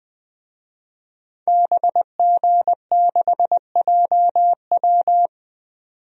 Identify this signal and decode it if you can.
Morse code, heard as a keyed tone, about 20 wpm, 700 Hz. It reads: BG6JW
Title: DXCC-100 Call Signs - 5 Characters - 20wpm